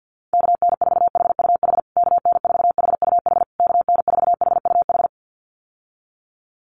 Morse code, heard as a keyed tone, about 50 wpm, 700 Hz. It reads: KN4HUH KN4HUH KN4HUH
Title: USA Call Signs Repeated 3x 50wpm